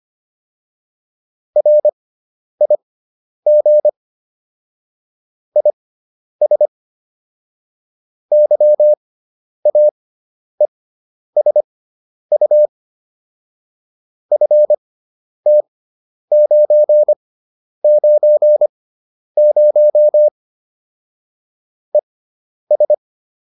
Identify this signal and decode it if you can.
Morse code, heard as a keyed tone, about 25 wpm, spaced as for 10 wpm, 600 Hz. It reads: RIG IS YAESU FT990 ES
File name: QSO304 10